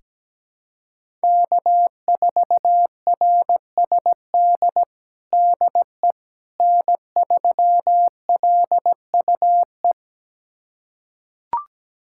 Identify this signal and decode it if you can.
Morse code, heard as a keyed tone, about 17 wpm, 700 Hz. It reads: K4RSD DE N3LUE
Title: USA Callsign DE Callsign - 5 Characters - 17wpm